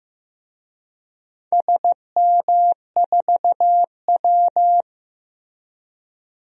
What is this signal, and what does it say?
Morse code, heard as a keyed tone, about 15 wpm, 700 Hz. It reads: SM4W
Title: DXCC-100 Call Signs - 4 Characters - Repeated 2x - 15wpm